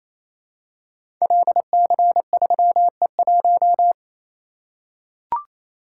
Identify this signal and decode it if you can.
Morse code, heard as a keyed tone, about 27 wpm, 700 Hz. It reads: LC3E1
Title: Code Groups - 5 Letter:Numbers - 28wpm